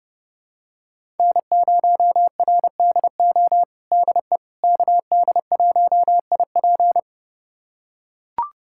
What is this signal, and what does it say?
Morse code, heard as a keyed tone, about 30 wpm, 700 Hz. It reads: N0RDO DE KD1IP
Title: USA Callsign DE Callsign - 5 Characters - 30wpm